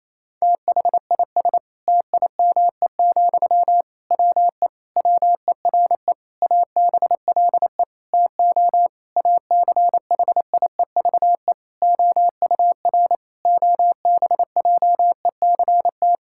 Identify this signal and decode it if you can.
Morse code, heard as a keyed tone, about 28 wpm, 700 Hz. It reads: THIS TIME, WE WERE ABLE TO ACHIEVE OUR OBJECT
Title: Sentences from Top 1000 Words - Review Part 6 - 28wpm